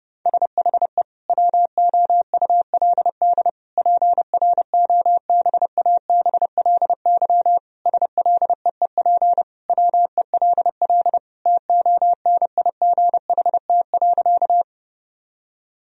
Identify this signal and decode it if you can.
Morse code, heard as a keyed tone, about 32 wpm, 700 Hz. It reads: SHE WOULD PROBABLY SLEEP WELL TONIGHT.
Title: Sentences from Top 1000 Words - Review Part 4 - 30wpm